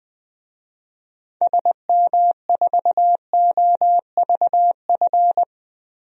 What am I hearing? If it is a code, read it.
Morse code, heard as a keyed tone, about 20 wpm, 700 Hz. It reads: SM4OVF
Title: DXCC-100 Call Signs - 6 Characters - 20wpm